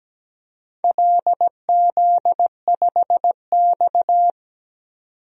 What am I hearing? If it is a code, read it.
Morse code, heard as a keyed tone, about 17 wpm, 700 Hz. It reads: LZ5X
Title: DXCC-100 Call Signs - 4 Characters - 17wpm